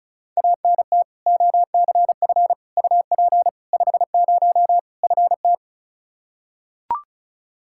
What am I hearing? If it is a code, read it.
Morse code, heard as a keyed tone, about 36 wpm, 700 Hz. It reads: ANT OCF UP 50 FT
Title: Practice QSO - Exchange 2 - Info - 35wpm